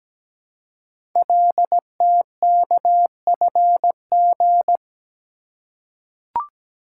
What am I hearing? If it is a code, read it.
Morse code, heard as a keyed tone, about 17 wpm, 700 Hz. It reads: LTKFG
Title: Code Groups - 5 Letters - 17wpm